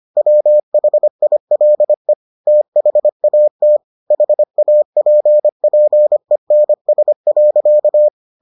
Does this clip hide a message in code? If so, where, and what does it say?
Yes: Morse code from 0.2 to 8.1 s, heard as a keyed tone, about 25 wpm, 600 Hz. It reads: WHILE THAT HAPPENS.